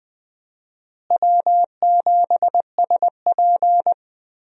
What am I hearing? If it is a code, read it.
Morse code, heard as a keyed tone, about 20 wpm, 700 Hz. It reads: W7SP
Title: USA Call Signs - 4 Characters - Repeated 3x - 20wpm